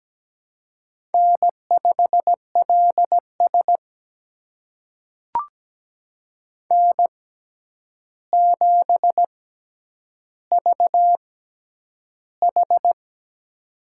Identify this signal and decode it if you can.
Morse code, heard as a keyed tone, about 17 wpm, spaced as for 9 wpm, 700 Hz. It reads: N5LS N7VH
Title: USA Call Signs - 4 Characters - 6x Character Spacing - 17wpm